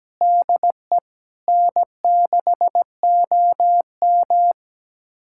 Morse code, heard as a keyed tone, about 17 wpm, 700 Hz. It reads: DE N6OM
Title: USA Callsign DE Callsign - 4 Characters - 17wpm